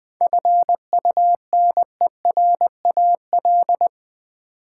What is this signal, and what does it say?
Morse code, heard as a keyed tone, about 20 wpm, 700 Hz. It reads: FUNERAL
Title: Seven Letter Words - 20wpm